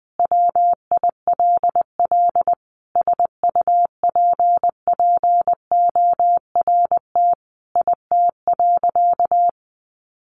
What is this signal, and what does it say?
Morse code, heard as a keyed tone, about 20 wpm, 700 Hz. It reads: WILL SUPPORT IT.
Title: Sentences from Top 1300 Words 20wpm